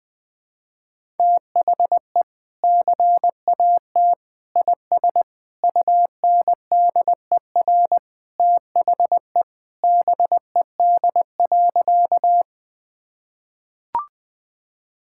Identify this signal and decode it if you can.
Morse code, heard as a keyed tone, about 20 wpm, 700 Hz. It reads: THE CAT IS UNDER THE BED.
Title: Sentences with 2-Word Phrases - 20wpm